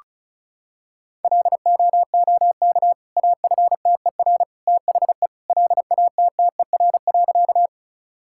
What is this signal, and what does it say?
Morse code, heard as a keyed tone, about 34 wpm, 700 Hz. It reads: LOOK AFTER THE LATTER.
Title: Sentences from Top 1600 Words 35wpm